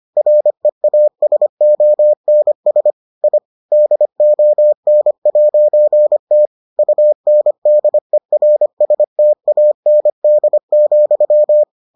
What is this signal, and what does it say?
Morse code, heard as a keyed tone, about 25 wpm, 600 Hz. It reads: REASONS I DON'T UNDERSTAND,